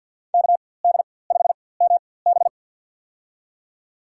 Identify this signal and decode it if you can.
Morse code, heard as a keyed tone, about 47 wpm, spaced as for 23 wpm, 700 Hz. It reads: KD5MB
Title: USA Call Signs - 5 Characters - 4x Character Spacing - 50wpm